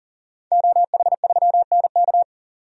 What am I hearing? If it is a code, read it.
Morse code, heard as a keyed tone, about 40 wpm, 700 Hz. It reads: OH3NK
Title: DXCC-100 Call Signs - 5 Characters - 40wpm